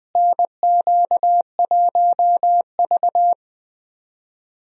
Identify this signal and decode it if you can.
Morse code, heard as a keyed tone, about 20 wpm, 700 Hz. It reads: NQ1V
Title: USA Call Signs - 4 Characters - Repeated 3x - 20wpm